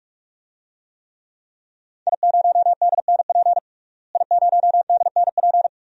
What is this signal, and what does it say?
Morse code, heard as a keyed tone, about 46 wpm, 700 Hz. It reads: I0DNP I0DNP
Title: DXCC-100 Call Signs - 5 Characters - Repeated 3x - 45wpm